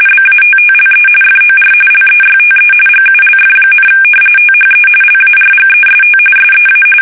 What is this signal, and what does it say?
RTTY, about 45.45 baud. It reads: BATTLE FOR LEYTE GULF, THE LARGEST SEA
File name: fsk850 / battleshipnj